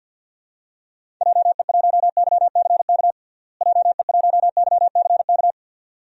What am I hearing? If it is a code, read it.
Morse code, heard as a keyed tone, about 53 wpm, 700 Hz. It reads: JE1YCK JE1YCK
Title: DXCC-100 Call Signs - 6 Characters - Repeated 3x - 50wpm